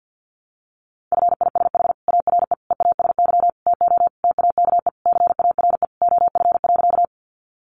Morse code, heard as a keyed tone, about 48 wpm, 700 Hz. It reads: FISH ARE EASY TO TAKE CARE OF.
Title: Easy Phrases 50wpm